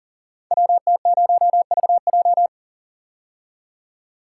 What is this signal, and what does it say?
Morse code, heard as a keyed tone, about 38 wpm, 690 Hz. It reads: WT0VJ
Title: Code Groups - 5 Letter:Numbers - 40wpm